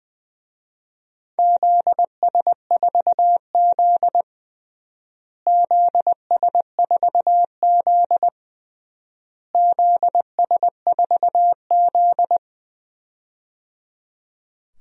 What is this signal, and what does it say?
Morse code, heard as a keyed tone, about 20 wpm, 700 Hz. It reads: ZS4Z ZS4Z ZS4Z
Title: DXCC-100 Call Signs - 4 Characters - Repeated 3x - 20wpm